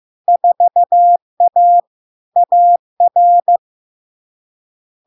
Morse code, heard as a keyed tone, about 15 wpm, 700 Hz. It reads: 4A AR
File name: LARK-FD-2022-Log-in-CW-15wpm-700-Hz